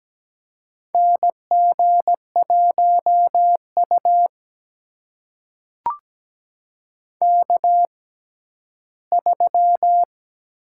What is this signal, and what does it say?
Morse code, heard as a keyed tone, about 17 wpm, spaced as for 12 wpm, 700 Hz. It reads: NG1U K3
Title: USA Call Signs - 4 Characters - 6x Character Spacing - 17wpm